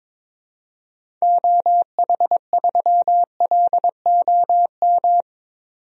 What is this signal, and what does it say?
Morse code, heard as a keyed tone, about 22 wpm, 700 Hz. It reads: OH3LOM
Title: DXCC-100 Call Signs - 6 Characters - 22wpm